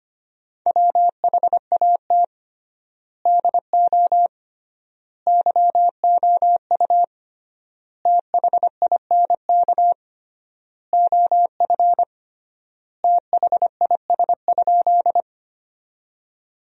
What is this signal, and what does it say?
Morse code, heard as a keyed tone, about 25 wpm, 700 Hz. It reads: WHAT DO YOU THINK OF THIS?
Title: 3x Word-Spacing - Sentences from Top 200 Words - 25wpm